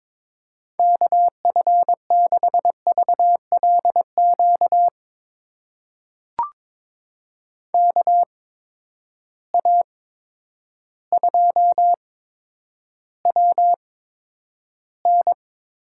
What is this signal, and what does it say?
Morse code, heard as a keyed tone, about 22 wpm, spaced as for 10 wpm, 700 Hz. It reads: KF6VLQ KA2WN
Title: USA Call Signs - 6 Characters - 8x Character Spacing - 22wpm